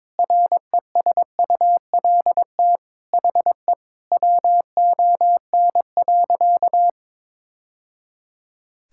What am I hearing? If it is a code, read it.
Morse code, heard as a keyed tone, about 22 wpm, 700 Hz. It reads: RESULT HE WON.
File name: practice / 22wpm / Sentences with 3-Word Phrases - 22wpm